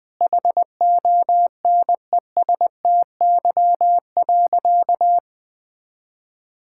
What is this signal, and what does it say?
Morse code, heard as a keyed tone, about 20 wpm, 700 Hz. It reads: HONESTY.
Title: Sentences with 8-Word Phrases - Spoken Beginning - 20wpm